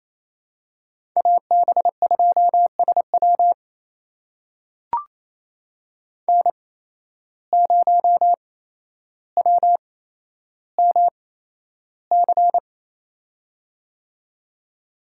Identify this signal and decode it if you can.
Morse code, heard as a keyed tone, about 28 wpm, spaced as for 12 wpm, 700 Hz. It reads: AB2SW N0WMC
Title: USA Call Signs - 5 Characters - 8x Character Spacing - 28wpm